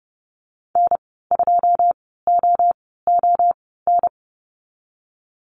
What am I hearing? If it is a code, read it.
Morse code, heard as a keyed tone, about 30 wpm, spaced as for 17 wpm, 700 Hz. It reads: N2OON
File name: Lesson 14g - Number 2 - Call Signs 30wpm